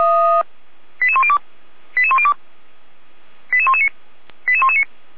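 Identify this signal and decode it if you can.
Five-tone selective call (ZVEI), heard as quick runs of five tones, tones of about 70 ms. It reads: 88102 88102 88119 88119
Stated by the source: ZVEI five-tone call received with an SDR, from the Feuerwehr Orlamünde audio server.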